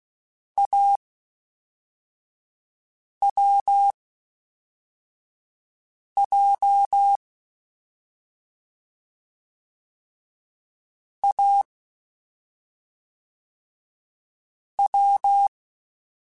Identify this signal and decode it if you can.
Morse code, heard as a keyed tone, about 16 wpm, spaced as for 4 wpm, 800 Hz. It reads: AWJ A W
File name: Difficult Combos / Difficult Combos-mp3s / AWJ1 @5